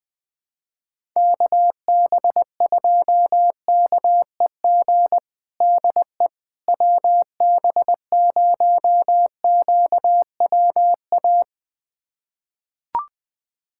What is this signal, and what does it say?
Morse code, heard as a keyed tone, about 20 wpm, 690 Hz. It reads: KB2KEG DE WB0QWA
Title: USA Callsign DE Callsign 20wpm